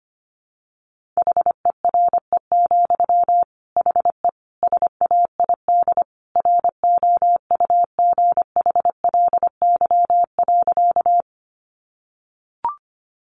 Morse code, heard as a keyed tone, about 25 wpm, 700 Hz. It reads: HERE, HE SAID ROUGHLY.
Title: Sentences from Top 1500 Words 25wpm